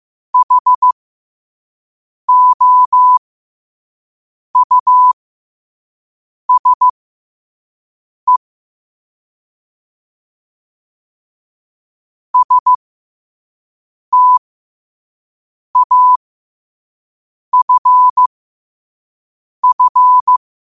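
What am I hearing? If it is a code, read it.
Morse code, heard as a keyed tone, about 15 wpm, spaced as for 5 wpm, 1000 Hz. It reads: HOUSE STAFF
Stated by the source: Morse code quote-of-the-day practice at 5 wpm.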